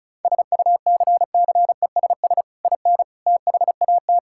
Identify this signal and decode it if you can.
Morse code, heard as a keyed tone, about 36 wpm, 700 Hz. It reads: SUCCESS IN THAT
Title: Sentences from Top 1000 Words - Review Part 5 - 35wpm